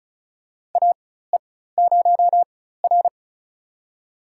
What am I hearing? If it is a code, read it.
Morse code, heard as a keyed tone, about 36 wpm, spaced as for 16 wpm, 700 Hz. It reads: AE0R
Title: USA Call Signs - 4 Characters - 4x Character Spacing - 35wpm